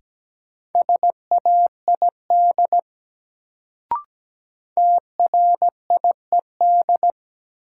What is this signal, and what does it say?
Morse code, heard as a keyed tone, about 17 wpm, 700 Hz. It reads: SAID TRIED
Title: Sets of 2 Words - New Word Drill - 17wpm